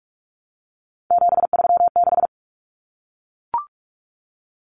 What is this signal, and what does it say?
Morse code, heard as a keyed tone, about 44 wpm, 700 Hz. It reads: 736
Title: Common Cut Numbers - 3-Digit 45wpm